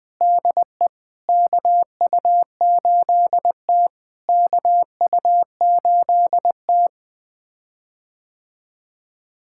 Morse code, heard as a keyed tone, about 20 wpm, 690 Hz. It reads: DE KU8T KU8T
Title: POTA QSO Practice - 20wpm